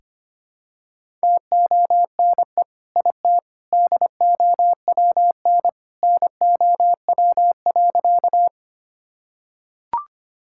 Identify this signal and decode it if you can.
Morse code, heard as a keyed tone, about 25 wpm, 700 Hz. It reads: TONE IT DOWN NOW.
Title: Sentences from Top 1600 Words 25wpm